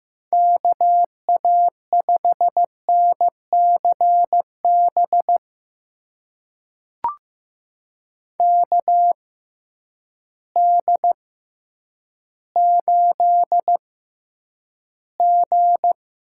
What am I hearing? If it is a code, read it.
Morse code, heard as a keyed tone, about 15 wpm, spaced as for 9 wpm, 700 Hz. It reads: KA5NCB KD8G
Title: USA Call Signs - 6 Characters - 6x Character Spacing - 15wpm